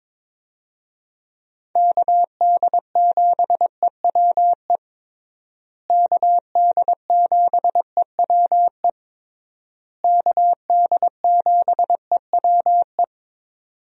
Morse code, heard as a keyed tone, about 22 wpm, 700 Hz. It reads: KD7EWE KD7EWE KD7EWE
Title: USA Call Signs - 6 Characters - Repeated 3x - 22wpm